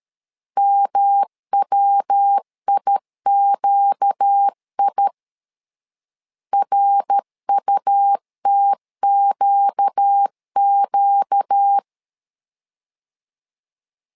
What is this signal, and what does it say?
Morse code, heard as a keyed tone, about 13 wpm, 800 Hz. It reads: MWIQI RUTQQ
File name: LEZIONI / 4 Q 12wpm 1833